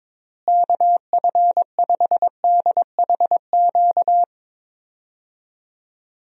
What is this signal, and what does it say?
Morse code, heard as a keyed tone, about 21 wpm, 700 Hz. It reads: KF5DHQ
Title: USA Call Signs - 6 Characters - Repeated 3x - 22wpm